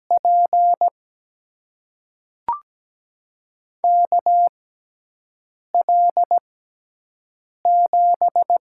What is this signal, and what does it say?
Morse code, heard as a keyed tone, about 17 wpm, spaced as for 6 wpm, 700 Hz. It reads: P KL7
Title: USA Call Signs - 5 Characters - 6x Character Spacing - 17wpm